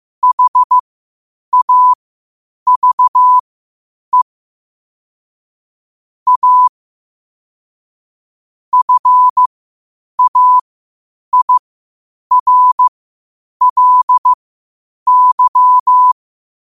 Morse code, heard as a keyed tone, about 15 wpm, spaced as for 8 wpm, 1000 Hz. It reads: HAVE A FAIRLY
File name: qotdfiles / 8wpm